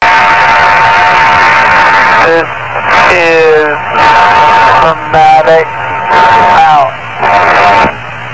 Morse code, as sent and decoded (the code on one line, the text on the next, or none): none